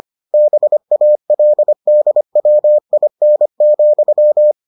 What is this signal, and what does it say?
Morse code, heard as a keyed tone, about 25 wpm, 600 Hz. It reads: BALDWIN,